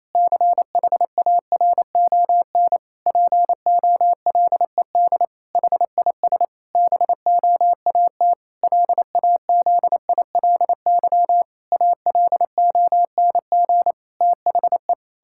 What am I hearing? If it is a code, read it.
Morse code, heard as a keyed tone, about 28 wpm, 700 Hz. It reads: CHARON POLED HIS BOAT LAZILY ALONG THE
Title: Pangram Challenge v2 - 28wpm